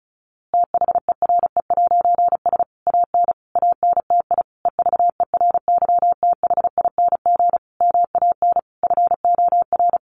Morse code, heard as a keyed tone, about 34 wpm, 700 Hz. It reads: THERE'S AN ANTI EVERYTHING MAN FOR
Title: A Sound Of Thunder 35wpm